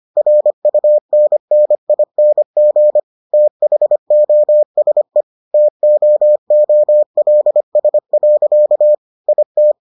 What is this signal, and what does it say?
Morse code, heard as a keyed tone, about 25 wpm, 600 Hz. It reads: RUNNING THOSE TOOLS. IT